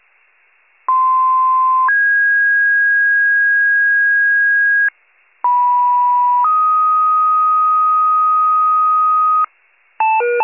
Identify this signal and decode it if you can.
Two-tone page, heard as a one-second tone, then a three-second tone, about 1040 then 1650 Hz, about 980 then 1250 Hz, with noise. It is heard from 0.9 to 9.5 s.